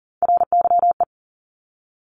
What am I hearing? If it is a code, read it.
Morse code, heard as a keyed tone, about 40 wpm, 700 Hz. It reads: RYE
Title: Multi-letter Phonemes Repeated 3x 40wpm